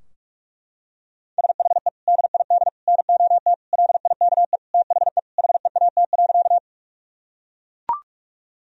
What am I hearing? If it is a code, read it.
Morse code, heard as a keyed tone, about 43 wpm, 700 Hz. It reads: SHE DID NOT LIKE THE HEAT.
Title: Sentences from Top 1000 Words - Review Part 3 - 45wpm